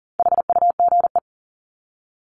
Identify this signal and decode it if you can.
Morse code, heard as a keyed tone, about 40 wpm, 700 Hz. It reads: HUGE
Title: Top 1001-1250 Words 40wpm